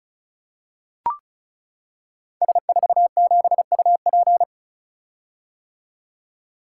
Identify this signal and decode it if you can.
Morse code, heard as a keyed tone, about 36 wpm, 700 Hz. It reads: S47UP
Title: Code Groups - 5 Letter:Numbers - 35wpm